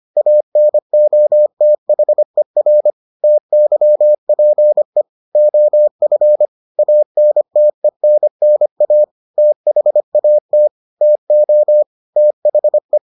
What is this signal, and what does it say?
Morse code, heard as a keyed tone, about 25 wpm, 600 Hz. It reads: ANOTHER TYPE OF ANTENNA THAT TO THE